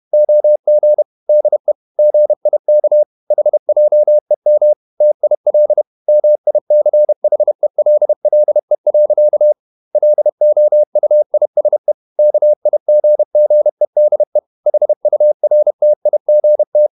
Morse code, heard as a keyed tone, about 31 wpm, 600 Hz. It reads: OG DE GIK HJEM TIL MICHELLE. LOUISE KIGGEDE HURTIGT
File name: Ferie QRQ 31wpm